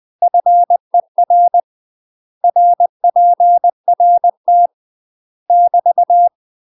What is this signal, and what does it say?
Morse code, heard as a keyed tone, about 20 wpm, 700 Hz. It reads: FER RPRT <BT>
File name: UK MASTER QSOs TRAINING / 20WPM / UK MASTER QSOs 0013